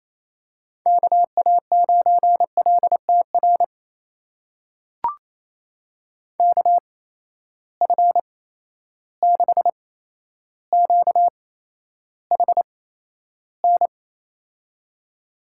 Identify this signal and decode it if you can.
Morse code, heard as a keyed tone, about 28 wpm, spaced as for 12 wpm, 700 Hz. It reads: KA9LTR KF6QHN
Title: USA Call Signs - 6 Characters - 8x Character Spacing - 28wpm